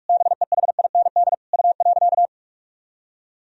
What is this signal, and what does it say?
Morse code, heard as a keyed tone, about 46 wpm, 700 Hz. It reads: BEHIND U.
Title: Sentences from Top 500 Words and Abbreviations 45wpm